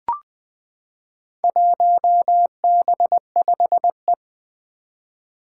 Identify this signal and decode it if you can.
Morse code, heard as a keyed tone, about 20 wpm, 700 Hz. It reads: 1B5E
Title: Four Challenging Letter-Number Combinations 20wpm